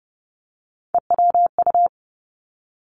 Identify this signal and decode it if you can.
Morse code, heard as a keyed tone, about 30 wpm, 700 Hz. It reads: EWU
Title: Three Challenging Letter-Number Combinations 30wpm